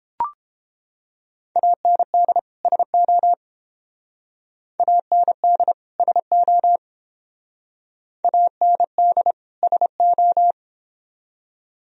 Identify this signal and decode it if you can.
Morse code, heard as a keyed tone, about 30 wpm, 700 Hz. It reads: AND SO AND SO AND SO